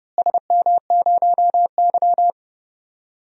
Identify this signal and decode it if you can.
Morse code, heard as a keyed tone, about 30 wpm, 700 Hz. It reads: SM0Y
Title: DXCC-100 Call Signs - 4 Characters - Repeated 2x - 30wpm